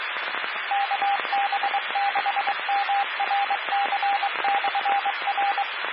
Morse code, heard as a keyed tone, about 23 wpm, 800 Hz, with radio noise. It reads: KB6MRC/R
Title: Claremont, CA Police, Amateur Radio